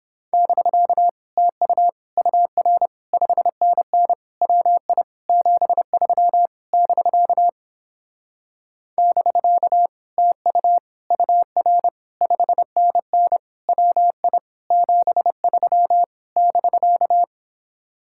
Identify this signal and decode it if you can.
Morse code, heard as a keyed tone, about 31 wpm, 700 Hz. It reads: <BK> TU UR 5NN WI 73 <BK> <BK> TU UR 5NN WI 73 <BK>